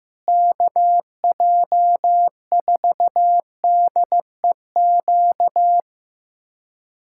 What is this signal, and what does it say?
Morse code, heard as a keyed tone, about 16 wpm, 700 Hz. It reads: KJ4DEQ
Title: USA Callsign DE Callsign 15wpm